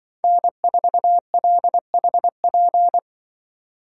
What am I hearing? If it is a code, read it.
Morse code, heard as a keyed tone, about 23 wpm, 700 Hz. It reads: N4LHP
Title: Speed Racing - USA 1x3 Call Signs 20wpm